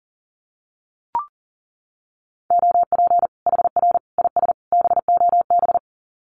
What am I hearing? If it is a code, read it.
Morse code, heard as a keyed tone, about 40 wpm, 700 Hz. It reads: OP HR IS BOB
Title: Common QSO Phrases 40wpm